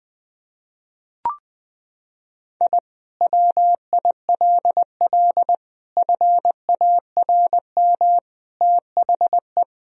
Morse code, heard as a keyed tone, about 21 wpm, 700 Hz. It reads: I WILL FARM THE
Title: Sentences from Top 1000 Words - Review Part 2 - 20wpm